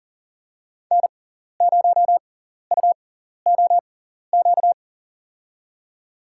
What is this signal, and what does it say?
Morse code, heard as a keyed tone, about 40 wpm, spaced as for 14 wpm, 700 Hz. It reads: N0UOQ
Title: USA Call Signs - 5 Characters - 6x Character Spacing - 40wpm